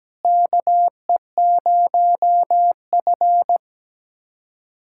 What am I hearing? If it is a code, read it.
Morse code, heard as a keyed tone, about 17 wpm, 700 Hz. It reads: KE0F
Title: USA Callsign DE Callsign - 4 Characters - 17wpm